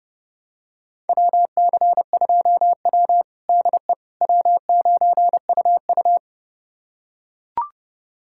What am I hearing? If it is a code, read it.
Morse code, heard as a keyed tone, about 30 wpm, 700 Hz. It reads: WC2W DE W9UU
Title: USA Callsign DE Callsign - 4 Characters - 30wpm